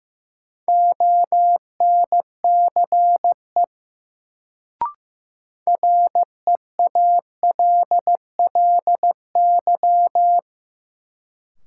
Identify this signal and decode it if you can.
Morse code, heard as a keyed tone, about 16 wpm, 700 Hz. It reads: ONCE REALLY
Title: Sets of 4 Words - New Word Drill - 15wpm